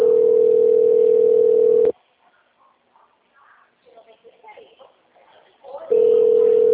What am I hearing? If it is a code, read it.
Ringback tone (North America), 440 + 480 Hz, with speech.
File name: out- -2017- - - .57243